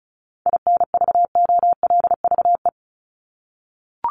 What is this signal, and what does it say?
Morse code, heard as a keyed tone, about 34 wpm, 700 Hz. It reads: INVOLVE
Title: Top 1001-1250 Words 35wpm